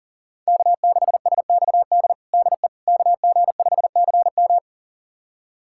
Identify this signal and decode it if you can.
Morse code, heard as a keyed tone, about 37 wpm, 700 Hz. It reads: K6SXD DE KG5CM
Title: USA Callsign DE Callsign - 5 Characters - 40wpm